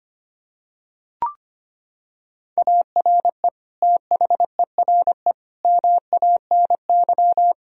Morse code, heard as a keyed tone, about 25 wpm, 700 Hz. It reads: ARE THERE MANY
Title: Sentences from Top 1000 Words - Review Part 1 - 25wpm